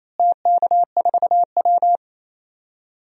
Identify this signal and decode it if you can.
Morse code, heard as a keyed tone, about 28 wpm, 700 Hz. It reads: TK4W
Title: DXCC-100 Call Signs - 4 Characters - Repeated 3x - 28wpm